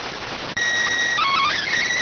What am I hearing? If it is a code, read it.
SSTV (slow-scan television), PD 290.